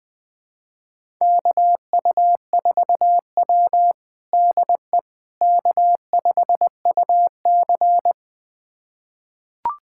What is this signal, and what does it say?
Morse code, heard as a keyed tone, about 21 wpm, 700 Hz. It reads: KU4W DE K5UC
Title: USA Callsign DE Callsign - 4 Characters - 20wpm